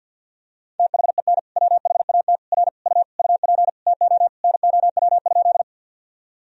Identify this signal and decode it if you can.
Morse code, heard as a keyed tone, about 51 wpm, 700 Hz. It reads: THEN WHAT R U UP TO NOW?
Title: Sentences from Top 100 Words and Abbreviations 50wpm